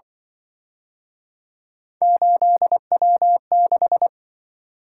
Morse code, heard as a keyed tone, about 25 wpm, 700 Hz. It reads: 8W6
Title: Speed-Racing - Three Letter-Number Combinations - 20wpm